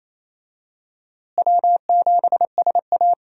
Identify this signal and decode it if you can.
Morse code, heard as a keyed tone, about 28 wpm, 700 Hz. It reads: W7SA
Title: USA Call Signs - 4 Characters - Repeated 2x - 28wpm